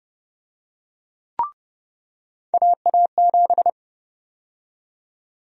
Morse code, heard as a keyed tone, about 32 wpm, 700 Hz. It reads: AA7
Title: USA Call Sign Prefixes - 30wpm